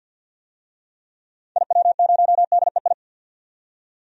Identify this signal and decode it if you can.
Morse code, heard as a keyed tone, about 52 wpm, 700 Hz. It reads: IW0DEI
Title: DXCC-100 Call Signs - 6 Characters - 50wpm